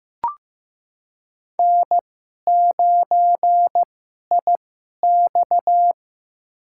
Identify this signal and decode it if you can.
Morse code, heard as a keyed tone, about 15 wpm, spaced as for 11 wpm, 700 Hz. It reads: N9IX
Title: USA Call Signs - 4 Characters - 2x Character Spacing - 15wpm